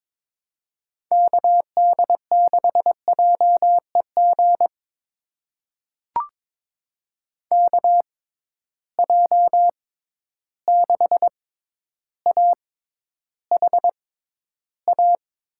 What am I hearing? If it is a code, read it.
Morse code, heard as a keyed tone, about 22 wpm, spaced as for 11 wpm, 700 Hz. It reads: KD6JEG KJ6AHA
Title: USA Call Signs - 6 Characters - 6x Character Spacing - 22wpm